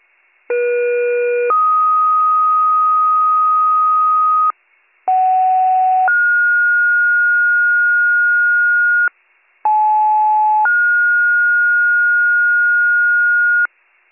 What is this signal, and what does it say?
Two-tone page, heard as a one-second tone, then a three-second tone, about 480 then 1250 Hz, about 750 then 1430 Hz, about 860 then 1430 Hz, with noise. It is heard from 0.5 to 13.7 s.